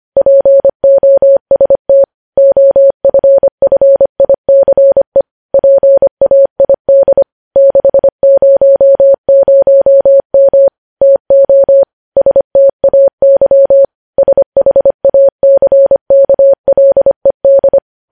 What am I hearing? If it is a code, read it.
Morse code, heard as a keyed tone, about 25 wpm, 550 Hz. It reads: POST OFFICE PAID 600M TO STAY SHACKLED
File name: podcast / News headlines-1x-25x25